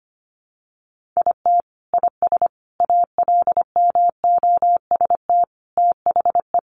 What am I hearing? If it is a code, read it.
Morse code, heard as a keyed tone, about 25 wpm, 700 Hz. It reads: IT IS ALMOST THE
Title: Sentences from Top 1400 Words 25wpm